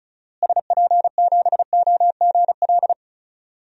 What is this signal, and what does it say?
Morse code, heard as a keyed tone, about 36 wpm, 700 Hz. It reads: SP7OGL
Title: DXCC-100 Call Signs - 6 Characters - Repeated 2x - 35wpm